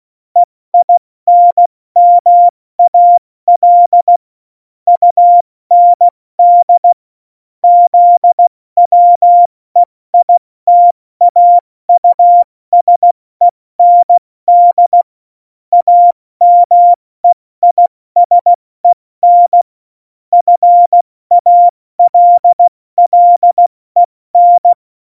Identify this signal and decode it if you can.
Morse code, heard as a keyed tone, about 16 wpm, 700 Hz. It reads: EINMAL UND ZWEITAUSEND AMEISEN FALLEN
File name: Morsekurs / Witze / Witz 03.Juli 201, 70 F80